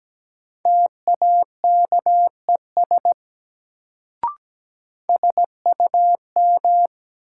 Morse code, heard as a keyed tone, about 17 wpm, 700 Hz. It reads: TAKES SUM
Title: Sentences from Top 1000 Words - Review Part 6 - New Word Drill - 17wpm